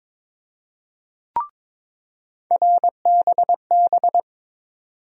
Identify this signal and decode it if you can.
Morse code, heard as a keyed tone, about 22 wpm, 700 Hz. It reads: RBB
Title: USA Call Sign Postfixes - 3 Letters 22wpm